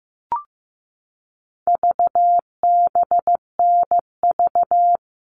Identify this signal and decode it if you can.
Morse code, heard as a keyed tone, about 14 wpm, 700 Hz. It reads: VBNV
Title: All Cut Numbers - 4-Digit 15wpm